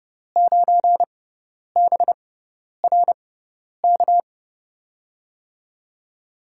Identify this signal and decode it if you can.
Morse code, heard as a keyed tone, about 30 wpm, spaced as for 10 wpm, 700 Hz. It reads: 9BRK
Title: USA Call Signs - 6x Character Spacing - 30wpm